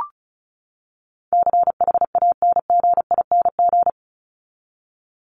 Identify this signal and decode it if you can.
Morse code, heard as a keyed tone, about 34 wpm, 700 Hz. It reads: CHANGING
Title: ING Suffix - Top 100 Words with Base Words 35wpm